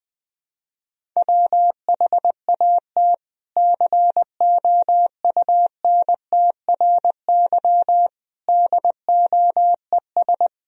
Morse code, heard as a keyed tone, about 20 wpm, 700 Hz. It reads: WHAT COUNTRY DOES
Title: Sentences from Top 300 Words and Abbreviations 20wpm